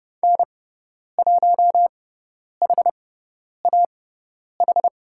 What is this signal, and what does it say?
Morse code, heard as a keyed tone, about 30 wpm, spaced as for 10 wpm, 690 Hz. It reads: N1HAH
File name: Lesson 18g - Letter m - Call Signs 30wpm